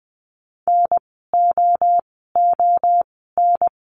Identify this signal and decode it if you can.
Morse code, heard as a keyed tone, about 20 wpm, spaced as for 14 wpm, 700 Hz. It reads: NOON
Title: 030 Lesson 6f - Letter i - Words 20wpm